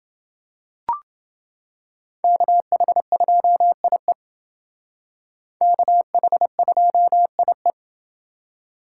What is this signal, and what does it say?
Morse code, heard as a keyed tone, about 28 wpm, 700 Hz. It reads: KH2IE KH2IE